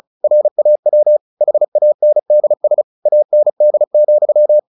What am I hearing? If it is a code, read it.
Morse code, heard as a keyed tone, about 34 wpm, 600 Hz. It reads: RAW HANDS AND,
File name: the old man and the sea 2 35wpm